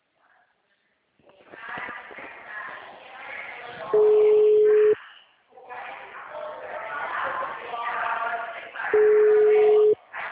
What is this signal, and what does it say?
Ringback tone (Europe, 425 Hz), with speech.